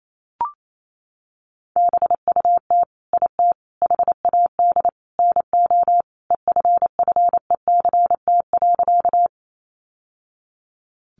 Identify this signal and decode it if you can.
Morse code, heard as a keyed tone, about 28 wpm, 700 Hz. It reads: BUT IT HAD NO EFFECT.
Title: Sentences from Top 1000 Words 28wpm